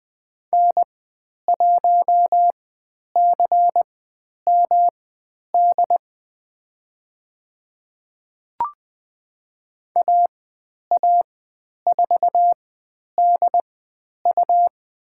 Morse code, heard as a keyed tone, about 20 wpm, spaced as for 10 wpm, 700 Hz. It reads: N1CMD AA4DU
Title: 109 Lesson 18g - Letter m - Call Signs 20wpm